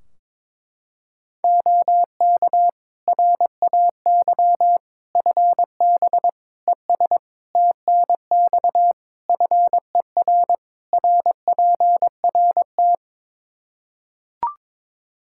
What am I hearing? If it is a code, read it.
Morse code, heard as a keyed tone, about 22 wpm, 700 Hz. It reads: OK RAY FB ES TNX FER RPRT
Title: Practice QSO - Exchange 2 - Info - 22wpm